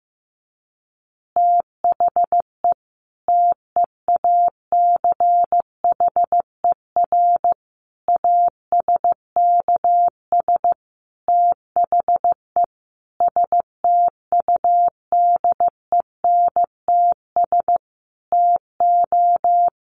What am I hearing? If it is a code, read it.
Morse code, heard as a keyed tone, about 15 wpm, 700 Hz. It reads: THE TEACHER ASKS THE STUDENTS TO
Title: Additional Easy Phrases 15wpm